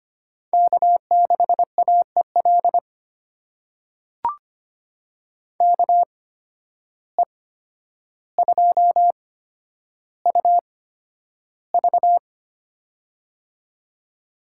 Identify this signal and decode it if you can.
Morse code, heard as a keyed tone, about 25 wpm, spaced as for 11 wpm, 700 Hz. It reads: K6AEL KE2UV
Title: USA Call Signs - 5 Characters - 8x Character Spacing - 25wpm